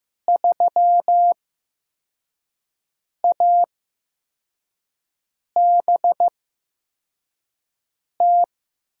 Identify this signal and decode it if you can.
Morse code, heard as a keyed tone, about 15 wpm, spaced as for 4 wpm, 700 Hz. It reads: 3ABT
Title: USA Call Signs - 6 Characters - 8x Character Spacing - 15wpm